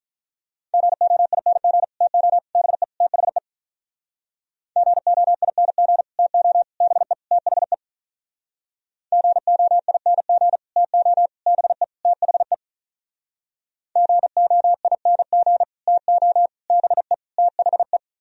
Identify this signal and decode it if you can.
Morse code, heard as a keyed tone, about 45 wpm, 700 Hz. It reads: GOING TO BE THE GOING TO BE THE GOING TO BE THE GOING TO BE THE